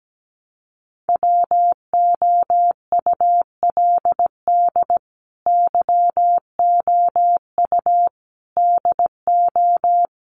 Morse code, heard as a keyed tone, about 17 wpm, 700 Hz. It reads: WOULD YOU DO
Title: Sentences from Top 100 Words 17wpm